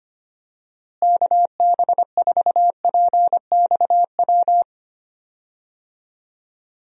Morse code, heard as a keyed tone, about 25 wpm, 700 Hz. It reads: KB4PXW
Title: USA Call Signs - 6 Characters - Repeated 2x - 25wpm